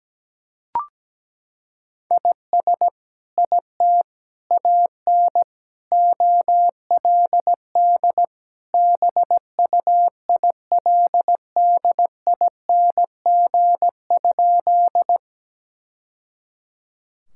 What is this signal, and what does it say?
Morse code, heard as a keyed tone, about 17 wpm, 700 Hz. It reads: IS IT AN OLD BUILDING?
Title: Sentences from Top 1000 Words - Review Part 5 - 17wpm